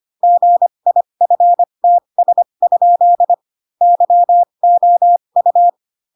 Morse code, heard as a keyed tone, about 25 wpm, spaced as for 22 wpm, 700 Hz. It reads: GIFTS? YOU
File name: Pinocchio / Pinocchio ch2 18 25 ik1hge